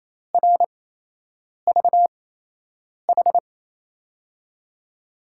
Morse code, heard as a keyed tone, about 28 wpm, spaced as for 8 wpm, 700 Hz. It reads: RVH
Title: USA Call Signs - 6 Characters - 8x Character Spacing - 28wpm